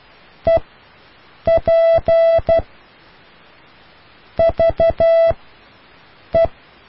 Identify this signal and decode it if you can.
Morse code, heard as a keyed tone, about 12 wpm, 660 Hz. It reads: E P V E